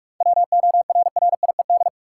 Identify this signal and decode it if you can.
Morse code, heard as a keyed tone, about 47 wpm, 700 Hz. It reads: WORRIED
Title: Seven Letter Words - 45wpm